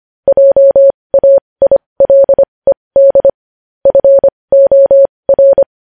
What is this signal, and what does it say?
Morse code, heard as a keyed tone, about 25 wpm, spaced as for 20 wpm, 550 Hz. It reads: JAILED FOR
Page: news headlines in morse code at 20 wpm